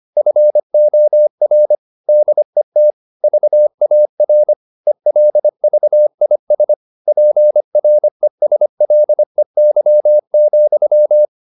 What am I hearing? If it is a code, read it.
Morse code, heard as a keyed tone, about 25 wpm, 600 Hz. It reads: FOR DET VAR ELVIS PRESLEY,
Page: Ferie 25 wpm